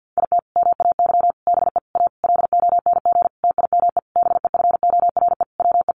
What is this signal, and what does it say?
Morse code, heard as a keyed tone, about 49 wpm, 700 Hz. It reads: IT MAY BE A LONG TIME BEFORE WE
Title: Sentences from Top 1100 Words 50wpm